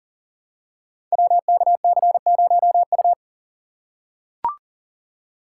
Morse code, heard as a keyed tone, about 40 wpm, 700 Hz. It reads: WKC0U
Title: Code Groups - 5 Letter:Numbers - 40wpm